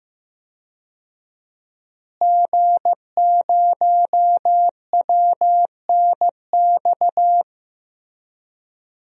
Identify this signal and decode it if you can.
Morse code, heard as a keyed tone, about 15 wpm, 700 Hz. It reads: G0WNX
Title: DXCC-100 Call Signs - 5 Characters - Repeated 3x - 15wpm